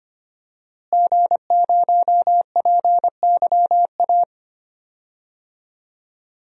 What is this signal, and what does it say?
Morse code, heard as a keyed tone, about 25 wpm, 700 Hz. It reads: G0PYA
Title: DXCC-100 Call Signs - 5 Characters - Repeated 3x - 25wpm